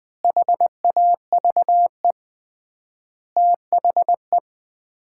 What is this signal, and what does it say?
Morse code, heard as a keyed tone, about 20 wpm, 700 Hz. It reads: HAVE THE
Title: 3x Word-Spacing - Sets of 5 Words - 20wpm